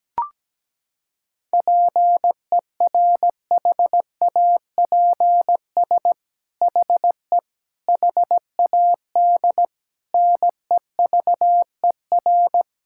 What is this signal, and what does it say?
Morse code, heard as a keyed tone, about 17 wpm, 700 Hz. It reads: PERHAPS HE HAD NEVER
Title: Sentences from Top 1600 Words 17wpm